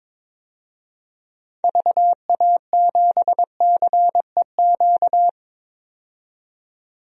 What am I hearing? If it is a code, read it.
Morse code, heard as a keyed tone, about 22 wpm, 700 Hz. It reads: VA7CEQ
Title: DXCC-100 Call Signs - 6 Characters - 22wpm